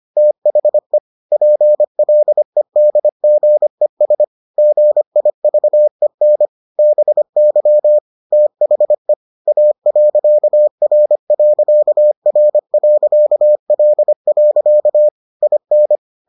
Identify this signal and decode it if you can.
Morse code, heard as a keyed tone, about 25 wpm, 600 Hz. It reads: THE PLEDGES GIVEN BY THE A.R.R.L. IN